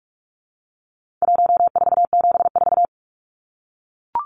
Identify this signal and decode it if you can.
Morse code, heard as a keyed tone, about 44 wpm, 700 Hz. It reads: 1474
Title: Numbers - 4-Digit 45wpm